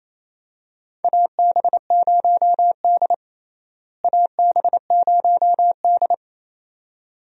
Morse code, heard as a keyed tone, about 28 wpm, 700 Hz. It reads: AB0D AB0D
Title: USA Call Signs - 4 Characters - Repeated 2x - 28wpm